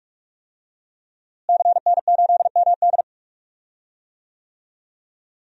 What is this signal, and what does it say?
Morse code, heard as a keyed tone, about 42 wpm, 700 Hz. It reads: CN8MD
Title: DXCC-100 Call Signs - 5 Characters - Repeated 2x - 45wpm